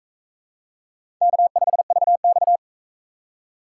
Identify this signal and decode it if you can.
Morse code, heard as a keyed tone, about 38 wpm, 700 Hz. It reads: K5VX